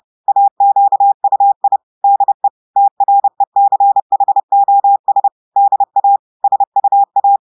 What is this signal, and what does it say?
Morse code, heard as a keyed tone, about 30 wpm, 800 Hz. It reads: AQUI DE TRECHOS DA SUA
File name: audios / portugais / portugais texte en clair 1 vitesse 30 mots minutes